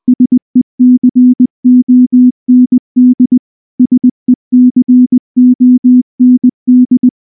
Morse code, heard as a keyed tone, about 19 wpm, 250 Hz. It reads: SECOND SECOND
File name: morse-training-20WPM